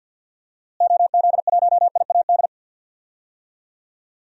Morse code, heard as a keyed tone, about 52 wpm, 700 Hz. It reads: OZ1IAD
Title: DXCC-100 Call Signs - 6 Characters - Repeated 2x - 50wpm